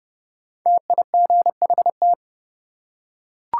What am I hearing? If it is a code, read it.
Morse code, heard as a keyed tone, about 30 wpm, 700 Hz. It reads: TIGHT
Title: Five Letter Words - 30wpm